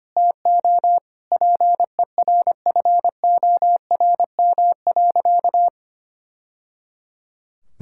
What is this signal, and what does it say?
Morse code, heard as a keyed tone, about 25 wpm, 700 Hz. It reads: TO PERFORM.
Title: Sentences from Top 1000 Words - Review Part 5 - 25wpm